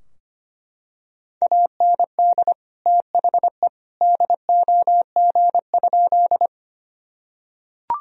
Morse code, heard as a keyed tone, about 25 wpm, 700 Hz. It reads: AND THE DOG?
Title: Sentences from Top 1000 Words - Review Part 2 - 25wpm